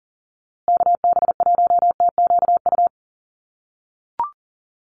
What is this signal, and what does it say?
Morse code, heard as a keyed tone, about 40 wpm, 700 Hz. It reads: KB1TQU
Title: USA Call Signs Repeated 2x 40wpm